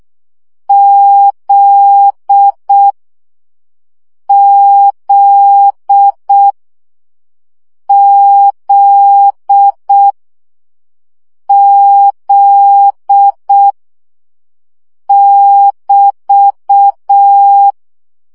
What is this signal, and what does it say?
Morse code, heard as a keyed tone, about 6 wpm, 800 Hz. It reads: Z Z Z Z <BT>
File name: CW gyakorlas z 5 baud